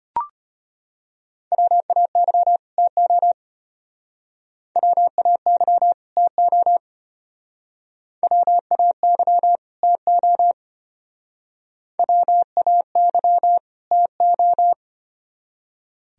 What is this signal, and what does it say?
Morse code, heard as a keyed tone, about 32 wpm, 700 Hz. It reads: WAY TO WAY TO WAY TO WAY TO